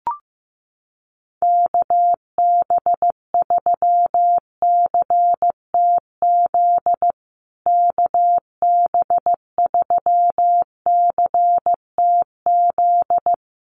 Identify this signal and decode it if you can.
Morse code, heard as a keyed tone, about 15 wpm, 700 Hz. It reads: KB3CTZ KB3CTZ
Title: USA Call Signs Repeated 3x 15wpm